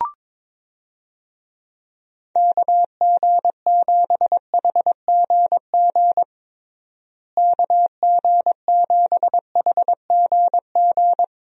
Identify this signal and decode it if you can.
Morse code, heard as a keyed tone, about 22 wpm, 700 Hz. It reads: KG7HGG KG7HGG
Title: USA Call Signs - 6 Characters - Repeated 3x - 22wpm